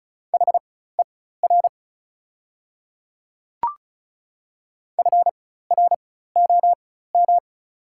Morse code, heard as a keyed tone, about 36 wpm, spaced as for 16 wpm, 690 Hz. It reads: HER FROM
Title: 4x Character Spacing - Top 100 Words - 35wpm